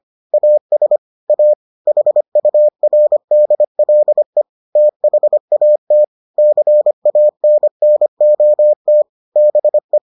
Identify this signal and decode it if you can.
Morse code, heard as a keyed tone, about 25 wpm, 600 Hz. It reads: AS A HURDLE THAT CANNOT BE